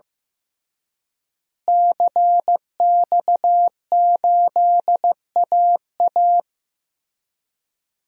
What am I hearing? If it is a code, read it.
Morse code, heard as a keyed tone, about 15 wpm, 700 Hz. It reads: CX8AA
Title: DXCC-100 Call Signs - 5 Characters - Repeated 3x - 15wpm